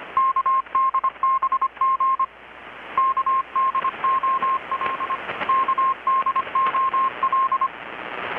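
Morse code, heard as a keyed tone, about 25 wpm, 1060 Hz. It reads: KDBG KDOL KDOL
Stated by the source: Morse from KPH's Night of Nights XX maritime radiotelegraphy event, received in Chiapas, Mexico about 3,800 km away on a Kenwood R-600 with a grounded 10 m long-wire antenna.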